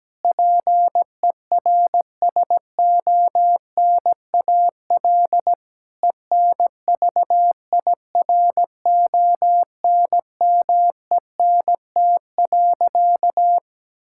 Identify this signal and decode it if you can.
Morse code, heard as a keyed tone, about 17 wpm, 700 Hz. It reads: PERSONAL ENVIRONMENT.
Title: Sentences from Top 1000 Words - Review Part 5 - 17wpm